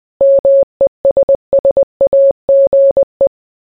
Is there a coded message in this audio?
Yes: Morse code from 0.2 to 3.3 s, heard as a keyed tone, about 20 wpm, 550 Hz. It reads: MESSAGE